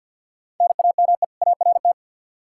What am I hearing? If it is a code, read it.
Morse code, heard as a keyed tone, about 52 wpm, 700 Hz. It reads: NAME ART
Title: K1USN SST QSO Practice 50wpm